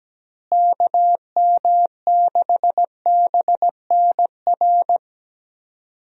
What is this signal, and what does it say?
Morse code, heard as a keyed tone, about 17 wpm, 700 Hz. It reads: KM6BNR
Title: USA Call Signs - 6 Characters - Repeated 3x - 17wpm